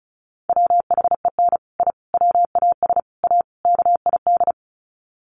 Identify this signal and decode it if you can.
Morse code, heard as a keyed tone, about 34 wpm, 700 Hz. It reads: WHEN I WAS A KID
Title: Sets of 5 Words 35wpm